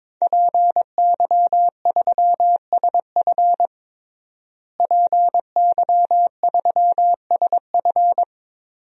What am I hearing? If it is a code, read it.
Morse code, heard as a keyed tone, about 22 wpm, 700 Hz. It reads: PY3SF PY3SF
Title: DXCC-100 Call Signs - 5 Characters - Repeated 2x - 22wpm